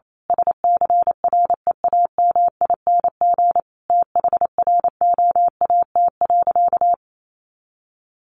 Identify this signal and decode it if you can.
Morse code, heard as a keyed tone, about 28 wpm, 700 Hz. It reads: SCREAMING THROAT.
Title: A Sound Of Thunder 28wpm